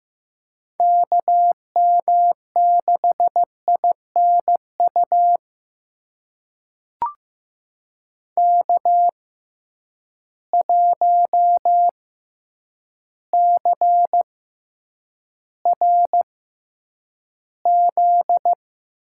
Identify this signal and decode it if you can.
Morse code, heard as a keyed tone, about 15 wpm, spaced as for 8 wpm, 700 Hz. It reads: KM6INU K1CRZ
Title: USA Call Signs - 6x Character Spacing - 15wpm